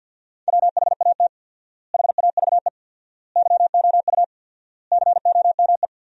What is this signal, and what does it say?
Morse code, heard as a keyed tone, about 50 wpm, 700 Hz. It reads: WHAT HAVE YOU COME
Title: 4x Word-Spacing - Sentences from Top 100 Words - 50wpm